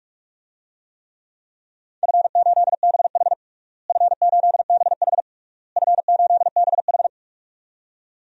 Morse code, heard as a keyed tone, about 41 wpm, 700 Hz. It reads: F8BH F8BH F8BH
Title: DXCC-100 Call Signs - 4 Characters - Repeated 3x - 45wpm